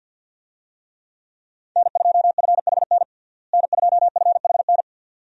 Morse code, heard as a keyed tone, about 52 wpm, 700 Hz. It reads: N2FHN N2FHN
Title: USA Call Signs - 5 Characters - Repeated 2x - 50wpm